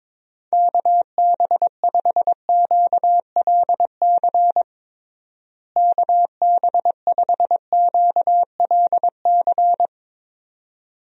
Morse code, heard as a keyed tone, about 21 wpm, 700 Hz. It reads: KB5QLC KB5QLC
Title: USA Call Signs - 6 Characters - Repeated 2x - 22wpm